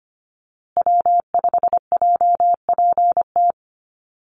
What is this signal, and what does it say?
Morse code, heard as a keyed tone, about 25 wpm, 700 Hz. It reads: W5JPT
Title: USA Call Signs Repeated 2x 25wpm